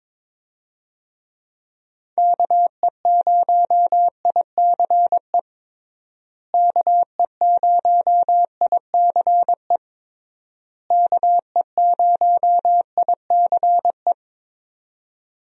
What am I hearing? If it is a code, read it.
Morse code, heard as a keyed tone, about 22 wpm, 700 Hz. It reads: KE0ICE KE0ICE KE0ICE
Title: USA Call Signs - 6 Characters - Repeated 3x - 22wpm